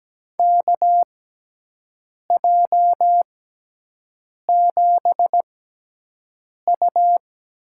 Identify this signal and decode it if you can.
Morse code, heard as a keyed tone, about 17 wpm, spaced as for 6 wpm, 700 Hz. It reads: KJ7U
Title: USA Call Signs - 5 Characters - 6x Character Spacing - 17wpm